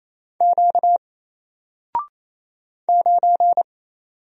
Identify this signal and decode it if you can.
Morse code, heard as a keyed tone, about 27 wpm, 700 Hz. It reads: Q 9
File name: Single-Letter-Number-28wpm